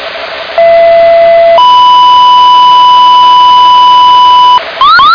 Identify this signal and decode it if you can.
Two-tone page, heard as a one-second tone, then a three-second tone, about 670 then 990 Hz, with noise. It is heard from 0.6 to 4.6 s.